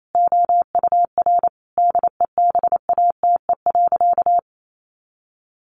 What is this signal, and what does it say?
Morse code, heard as a keyed tone, about 28 wpm, 700 Hz. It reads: OUR DEBATE.
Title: Sentences from Top 1500 Words 28wpm